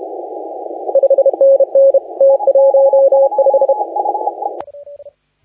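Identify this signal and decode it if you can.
Morse code, heard as a keyed tone, about 32 wpm, 550 Hz, with radio noise. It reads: 5NN T15
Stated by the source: Morse contest audio recorded by LA4C during the SAC CW 2018 contest.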